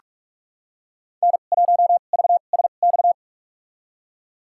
Morse code, heard as a keyed tone, about 41 wpm, spaced as for 31 wpm, 700 Hz. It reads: N1VSX
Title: USA Call Signs - 2x Character Spacing - 45wpm